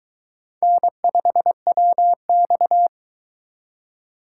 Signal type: Morse code

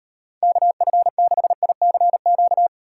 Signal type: Morse code

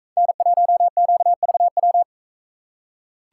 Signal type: Morse code

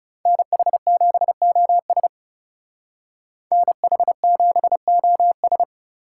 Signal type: Morse code

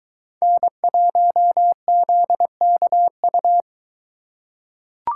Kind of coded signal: Morse code

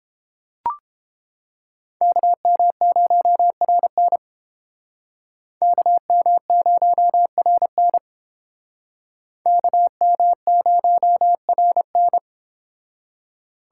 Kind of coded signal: Morse code

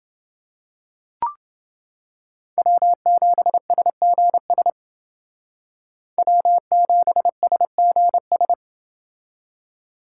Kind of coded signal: Morse code

